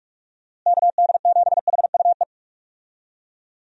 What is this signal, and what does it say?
Morse code, heard as a keyed tone, about 43 wpm, 700 Hz. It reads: KD7HUE
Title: USA Call Signs - 6 Characters - Repeated 3x - 45wpm